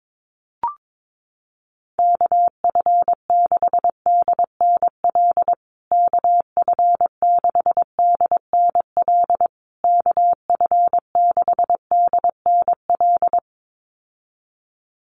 Morse code, heard as a keyed tone, about 22 wpm, 700 Hz. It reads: KF6DNL KF6DNL KF6DNL
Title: USA Call Signs Repeated 3x 22wpm